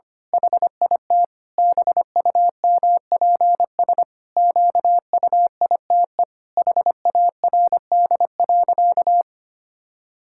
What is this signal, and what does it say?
Morse code, heard as a keyed tone, about 25 wpm, 700 Hz. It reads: HIT BUMPS QUITE HARD.
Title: Pangram Challenge v2 - 25wpm